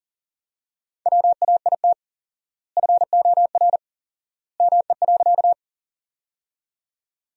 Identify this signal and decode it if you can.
Morse code, heard as a keyed tone, about 37 wpm, 700 Hz. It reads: WAIT FOR ME.
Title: 4x Word-Spacing - Sentences from Top 500 Words - 40wpm